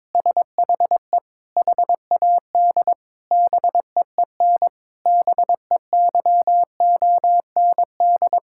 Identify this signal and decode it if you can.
Morse code, heard as a keyed tone, about 22 wpm, 700 Hz. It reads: SHE HAD BEEN BEYOND
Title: Sentences from Top 1000 Words - Review Part 6 - 22wpm